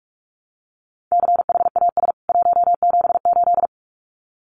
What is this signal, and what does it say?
Morse code, heard as a keyed tone, about 44 wpm, 700 Hz. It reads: CHAS 178
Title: CWT Exchange 45wpm